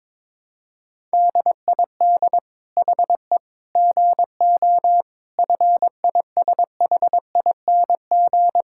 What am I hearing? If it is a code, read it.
Morse code, heard as a keyed tone, about 21 wpm, 700 Hz. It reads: DID HE GO FISHING
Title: Sentences from Top 1000 Words - Review Part 5 - 22wpm